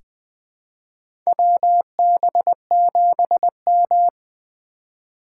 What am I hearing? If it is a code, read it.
Morse code, heard as a keyed tone, about 20 wpm, 700 Hz. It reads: WB7M
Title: USA Call Signs - 4 Characters - 2x Character Spacing - 20wpm